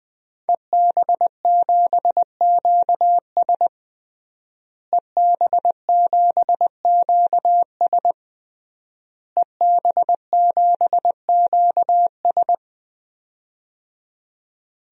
Morse code, heard as a keyed tone, about 20 wpm, 700 Hz. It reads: EB7QS EB7QS EB7QS
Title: DXCC-100 Call Signs - 5 Characters - Repeated 3x - 20wpm